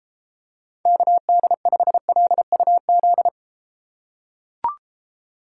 Morse code, heard as a keyed tone, about 33 wpm, 700 Hz. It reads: KD5LUZ